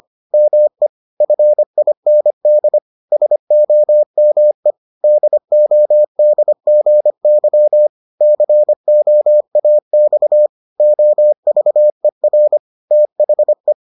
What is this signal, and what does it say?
Morse code, heard as a keyed tone, about 25 wpm, 600 Hz. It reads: ME FIND SOME DODGY COAX OVER THE